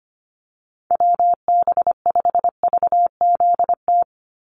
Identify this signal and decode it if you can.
Morse code, heard as a keyed tone, about 25 wpm, 700 Hz. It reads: WB5VZT
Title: USA Call Signs 25wpm